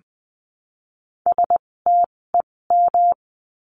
Morse code, heard as a keyed tone, about 20 wpm, spaced as for 16 wpm, 700 Hz. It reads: STEM
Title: 256 Lesson 42a - Words 12-20wpm